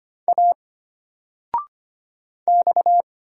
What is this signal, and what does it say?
Morse code, heard as a keyed tone, about 25 wpm, 700 Hz. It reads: A X
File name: Single-Letter-Number-25wpm